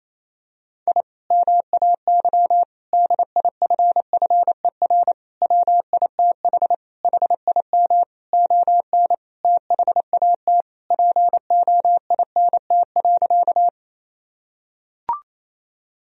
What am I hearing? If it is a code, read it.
Morse code, heard as a keyed tone, about 28 wpm, 700 Hz. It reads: I MAY DIFFER WITH HIM ON THAT POINT.
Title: Sentences from Top 1000 Words - Review Part 1 - 28wpm